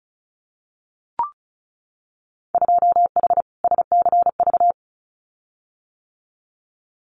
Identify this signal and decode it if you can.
Morse code, heard as a keyed tone, about 36 wpm, 700 Hz. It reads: 2H SCV
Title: Winter Field Day Exchanges Encore 35wpm